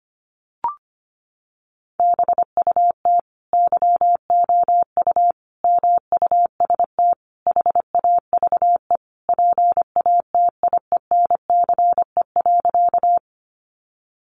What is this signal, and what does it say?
Morse code, heard as a keyed tone, about 25 wpm, 700 Hz. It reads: BUT YOU MUST HAVE PATIENCE.
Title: Sentences from Top 1300 Words 25wpm